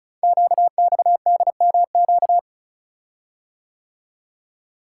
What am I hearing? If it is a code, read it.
Morse code, heard as a keyed tone, about 36 wpm, 700 Hz. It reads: QXDMQ
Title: Code Groups - 5 Letters - 35wpm